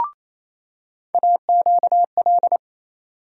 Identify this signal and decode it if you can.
Morse code, heard as a keyed tone, about 27 wpm, 700 Hz. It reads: AQL
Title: USA Call Sign Postfixes - 3 Letters 28wpm